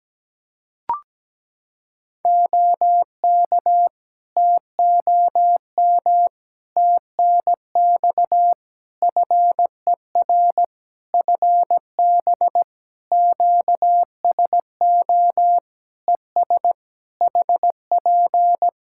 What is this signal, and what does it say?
Morse code, heard as a keyed tone, about 17 wpm, 700 Hz. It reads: OK TOM TNX FER FB QSO ES HP
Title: Practice QSO - Exchange 4 - Ending - 17wpm